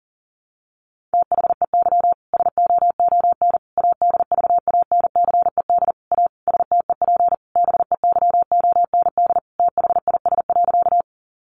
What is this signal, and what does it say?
Morse code, heard as a keyed tone, about 40 wpm, 700 Hz. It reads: THEY SOON ADVANCED A STEP BEYOND THIS.
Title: Sentences from Top 1200 Words 40wpm